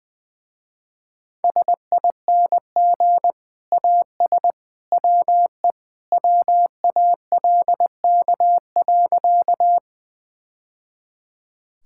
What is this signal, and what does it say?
Morse code, heard as a keyed tone, about 20 wpm, 700 Hz. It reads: SING AS WE WALK.
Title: Sentences with 2-Word Phrases - Spoken Beginning - 20wpm